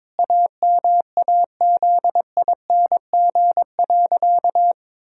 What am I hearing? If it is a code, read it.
Morse code, heard as a keyed tone, about 22 wpm, 700 Hz. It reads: AMAZING.
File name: practice / 22wpm / Sentences with 3-Word Phrases - 22wpm